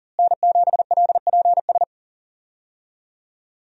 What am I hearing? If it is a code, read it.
Morse code, heard as a keyed tone, about 40 wpm, 700 Hz. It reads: N7LPS
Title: USA Call Signs - 5 Characters - Repeated 3x - 40wpm